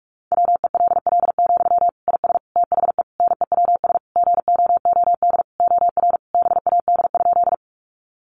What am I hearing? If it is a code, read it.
Morse code, heard as a keyed tone, about 44 wpm, 700 Hz. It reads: WELL, IS THE NEWS GOOD OR BAD?
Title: Sentences from Top 800 Words 45wpm